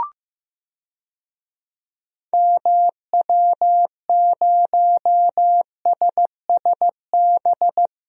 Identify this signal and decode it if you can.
Morse code, heard as a keyed tone, about 15 wpm, 700 Hz. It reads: MW0SSB
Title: DXCC-100 Call Signs - 6 Characters - Repeated 3x - 15wpm